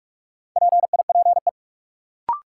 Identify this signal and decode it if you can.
Morse code, heard as a keyed tone, about 46 wpm, 700 Hz. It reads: PIPE
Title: Sentences from Top 1500 Words - New Word Drill - 45wpm